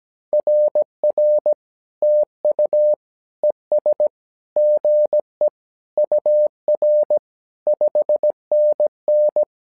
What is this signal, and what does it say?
Morse code, heard as a keyed tone, about 17 wpm, 600 Hz. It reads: RR TU ES GE UR 5NN
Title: SOTA QSO Practice 17wpm